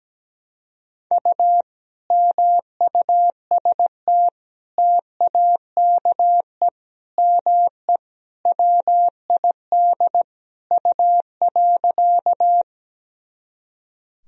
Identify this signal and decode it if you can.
Morse code, heard as a keyed tone, about 17 wpm, 700 Hz. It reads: U MUST TAKE ME WID U.
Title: Sentences from Top 200 Words and Abbreviations 17wpm